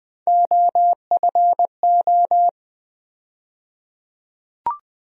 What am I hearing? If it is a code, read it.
Morse code, heard as a keyed tone, about 20 wpm, 700 Hz. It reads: OFO
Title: USA Call Sign Postfixes - 3 Letters 20wpm